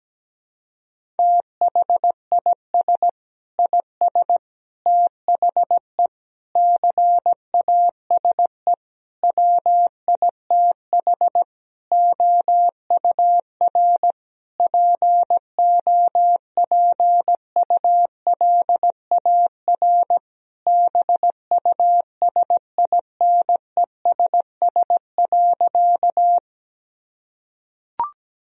Morse code, heard as a keyed tone, about 17 wpm, 700 Hz. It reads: THIS IS THE CASE WITH OUR POPULAR BUSINESS.
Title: Sentences from Top 1000 Words - Review Part 3 - 17wpm